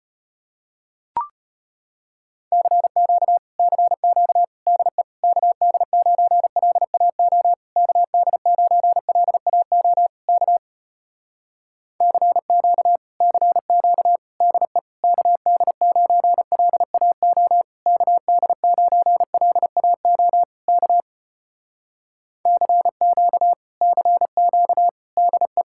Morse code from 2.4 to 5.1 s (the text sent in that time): CQ CQ DE